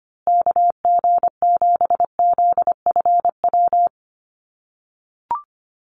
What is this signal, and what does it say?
Morse code, heard as a keyed tone, about 25 wpm, 700 Hz. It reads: KG7ZFW
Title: USA Call Signs Repeated 2x 25wpm